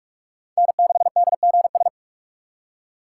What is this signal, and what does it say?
Morse code, heard as a keyed tone, about 46 wpm, 700 Hz. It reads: N6DGS
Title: USA Call Signs - 5 Characters - 4x Character Spacing - 45wpm